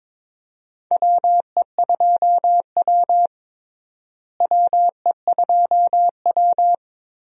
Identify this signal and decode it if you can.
Morse code, heard as a keyed tone, about 21 wpm, 700 Hz. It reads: WE2W WE2W
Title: USA Call Signs - 4 Characters - Repeated 2x - 22wpm